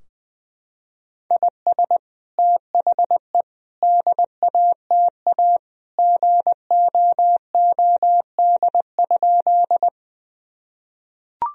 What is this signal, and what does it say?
Morse code, heard as a keyed tone, about 20 wpm, 700 Hz. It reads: IS THE DATA GOOD?
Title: Sentences from Top 1000 Words - Review Part 3 - 20wpm